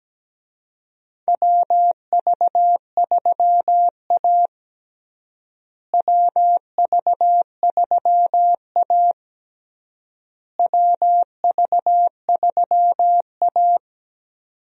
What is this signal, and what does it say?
Morse code, heard as a keyed tone, about 17 wpm, 700 Hz. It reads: WV3A WV3A WV3A
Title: USA Call Signs - 4 Characters - Repeated 3x - 17wpm